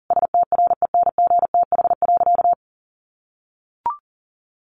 Morse code, heard as a keyed tone, about 40 wpm, 700 Hz. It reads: STRENGTH.
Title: Sentences from Top 1300 Words 40wpm